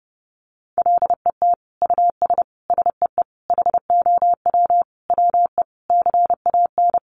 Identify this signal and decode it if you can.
Morse code, heard as a keyed tone, about 30 wpm, 700 Hz. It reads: LET US SEE HOW WE CAN
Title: Sentences from Top 1400 Words 30wpm